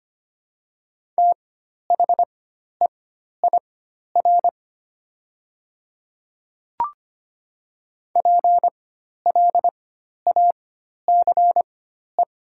Morse code, heard as a keyed tone, about 25 wpm, spaced as for 12 wpm, 700 Hz. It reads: THEIR PLACE
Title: 4x Character Spacing - Top 200 Words - 25wpm